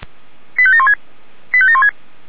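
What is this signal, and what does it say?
Five-tone selective call (ZVEI), heard as quick runs of five tones, tones of about 70 ms. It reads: 85716 85716
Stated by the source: Five-tone ZVEI selective call received on an SDR.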